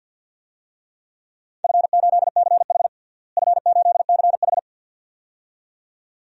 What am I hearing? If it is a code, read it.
Morse code, heard as a keyed tone, about 51 wpm, 700 Hz. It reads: F8CH F8CH
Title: DXCC-100 Call Signs - 4 Characters - Repeated 2x - 50wpm